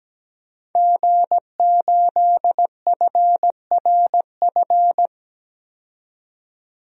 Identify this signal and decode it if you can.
Morse code, heard as a keyed tone, about 17 wpm, 700 Hz. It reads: G8FRF
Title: DXCC-100 Call Signs - 5 Characters - 17wpm